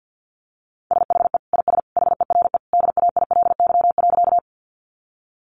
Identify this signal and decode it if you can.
Morse code, heard as a keyed tone, about 50 wpm, 700 Hz. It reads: SHE IS HERE DAILY.
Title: Sentences from Top 1100 Words 50wpm